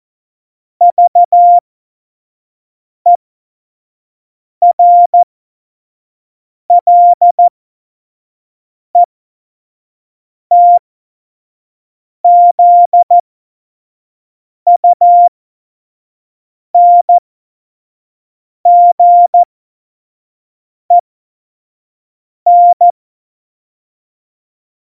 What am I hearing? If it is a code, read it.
Morse code, heard as a keyed tone, about 14 wpm, spaced as for 5 wpm, 700 Hz. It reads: VERLETZUNGEN
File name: 25 F70